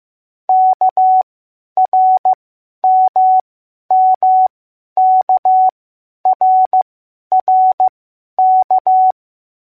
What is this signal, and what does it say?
Morse code, heard as a keyed tone, about 15 wpm, spaced as for 10 wpm, 750 Hz. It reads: KRMMKRRK